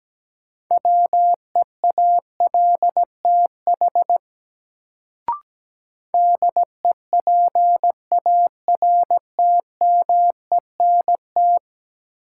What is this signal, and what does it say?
Morse code, heard as a keyed tone, about 17 wpm, 700 Hz. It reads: WEALTH DEPARTMENT
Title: Sentences from Top 1200 Words - New Word Drill Alt - 17wpm